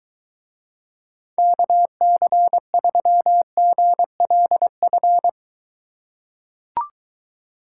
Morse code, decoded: KC3GLF

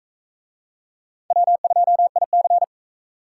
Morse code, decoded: W2IC

